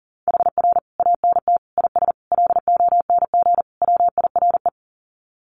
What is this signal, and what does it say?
Morse code, heard as a keyed tone, about 40 wpm, 700 Hz. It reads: HR ANT IS LONG WIRE
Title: Common QSO Phrases 40wpm